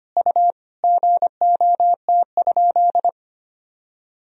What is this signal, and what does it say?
Morse code, heard as a keyed tone, about 25 wpm, 700 Hz. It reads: U GOT?
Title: Sentences from Top 300 Words and Abbreviations 25wpm